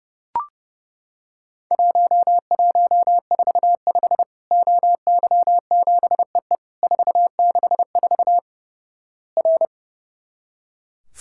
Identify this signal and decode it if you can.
Morse code, heard as a keyed tone, about 32 wpm, 700 Hz. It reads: 1145 OY7EE 464 R
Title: Worked All Europe DX Contest - QTC Exchange - 30wpm